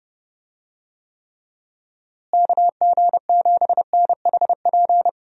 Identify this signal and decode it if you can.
Morse code, heard as a keyed tone, about 30 wpm, 700 Hz. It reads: KG7NHP
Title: USA Call Signs - 6 Characters - Repeated 3x - 30wpm